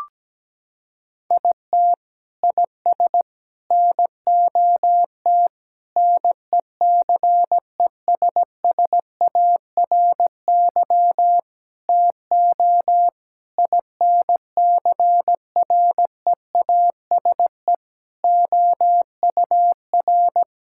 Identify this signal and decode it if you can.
Morse code, heard as a keyed tone, about 17 wpm, 700 Hz. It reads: IT IS NOT NECESSARY TO INCREASE OUR
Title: Sentences from Top 1000 Words - Review Part 6 - 17wpm